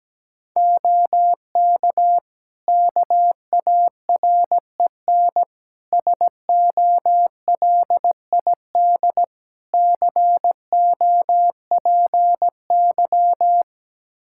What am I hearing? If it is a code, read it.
Morse code, heard as a keyed tone, about 17 wpm, 700 Hz. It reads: OK KAREN SOLID COPY
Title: Practice QSO - Exchange 3 - More Info - 17wpm